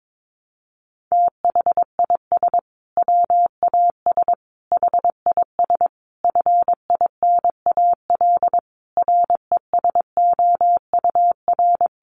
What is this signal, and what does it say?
Morse code, heard as a keyed tone, about 22 wpm, 700 Hz. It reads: THIS WAS HIS FINAL RESOUR
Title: Sentences from Top 1300 Words 22wpm